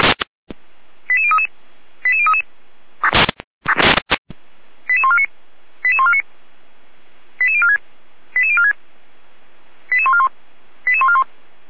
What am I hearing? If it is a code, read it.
Five-tone selective call (ZVEI), heard as quick runs of five tones, tones of about 70 ms. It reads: 88033 88033 88149 88149 88046 88046 88141 88141